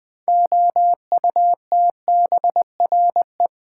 Morse code, heard as a keyed tone, about 20 wpm, 700 Hz. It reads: OUTBRE
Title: Build A Word - 3 Letters At A Time - 8-Letter Words - 20wpm